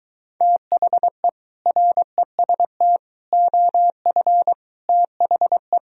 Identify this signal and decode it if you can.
Morse code, heard as a keyed tone, about 23 wpm, 700 Hz. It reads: THE REST OF THE